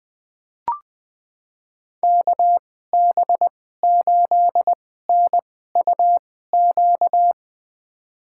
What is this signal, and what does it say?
Morse code, heard as a keyed tone, about 20 wpm, spaced as for 14 wpm, 700 Hz. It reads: KB8NUQ
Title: USA Call Signs - 6 Characters - 2x Character Spacing - 20wpm